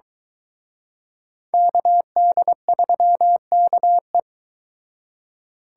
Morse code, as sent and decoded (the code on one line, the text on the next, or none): -.- -.. ...-- -.- .
KD3KE